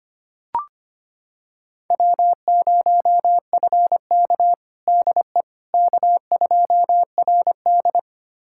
Morse code, heard as a keyed tone, about 25 wpm, 700 Hz. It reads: W0FK DE K2RD
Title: USA Callsign DE Callsign - 4 Characters - 25wpm